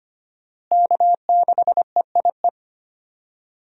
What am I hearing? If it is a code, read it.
Morse code, heard as a keyed tone, about 25 wpm, 700 Hz. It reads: K6EIE
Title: USA Call Signs - 5 Characters - Repeated 3x - 25wpm